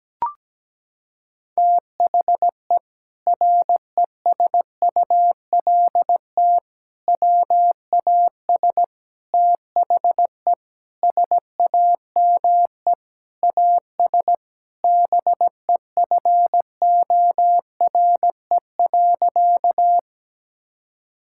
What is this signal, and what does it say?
Morse code, heard as a keyed tone, about 17 wpm, 700 Hz. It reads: THE RESULT WAS THE SAME AS BEFORE.
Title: Sentences from Top 1000 Words - Review Part 5 - 17wpm